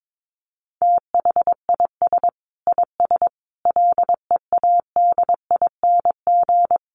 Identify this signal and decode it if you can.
Morse code, heard as a keyed tone, about 22 wpm, 700 Hz. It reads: THIS IS LEADING
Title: Sentences from Top 1100 Words 22wpm